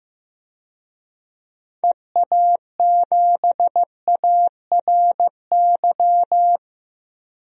Morse code, heard as a keyed tone, about 15 wpm, 700 Hz. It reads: EA7ARY
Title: DXCC-100 Call Signs - 6 Characters - Repeated 3x - 15wpm